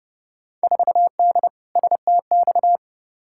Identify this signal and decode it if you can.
Morse code, heard as a keyed tone, about 29 wpm, 700 Hz. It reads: 4D STX
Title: Field Day Exchanges Encore - 30wpm